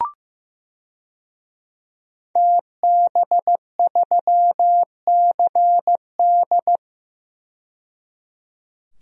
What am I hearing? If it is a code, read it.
Morse code, heard as a keyed tone, about 15 wpm, 700 Hz. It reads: TB3CD
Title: DXCC-100 Call Signs - 5 Characters - 15wpm